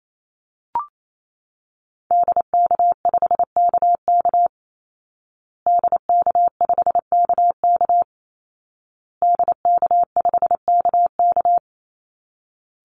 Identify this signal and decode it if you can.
Morse code, heard as a keyed tone, about 28 wpm, 700 Hz. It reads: DK5KK DK5KK DK5KK
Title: Call Signs Repeated 2x - 28wpm